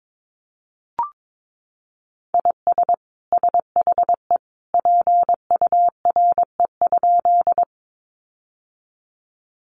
Morse code, heard as a keyed tone, about 22 wpm, 700 Hz. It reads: IS SHE PURE?
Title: Sentences from Top 1500 Words 22wpm